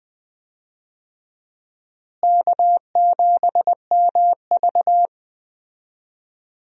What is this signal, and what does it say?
Morse code, heard as a keyed tone, about 20 wpm, 700 Hz. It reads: K7MV
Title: USA Call Signs - 4 Characters - Repeated 2x - 20wpm